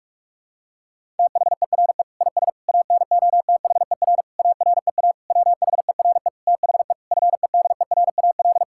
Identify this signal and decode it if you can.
Morse code, heard as a keyed tone, about 46 wpm, 700 Hz. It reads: THERE IS ANOTHER AREA WHERE THE FEDERAL
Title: Sentences from Top 1000 Words - Review Part 5 - 45wpm